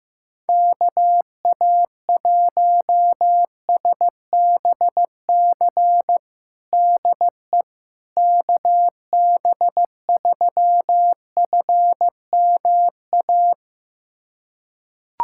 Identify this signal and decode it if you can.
Morse code, heard as a keyed tone, about 16 wpm, 700 Hz. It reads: KA1SBC DE KB3FMA
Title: USA Callsign DE Callsign - 6 Characters - 15wpm